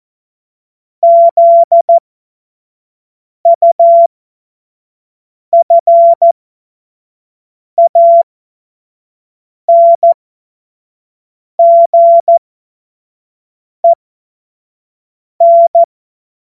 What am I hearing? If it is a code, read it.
Morse code, heard as a keyed tone, about 14 wpm, spaced as for 5 wpm, 680 Hz. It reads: ZUFANGEN
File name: Morsekurs / Witze / Witz 16. Mai 2023 25 F70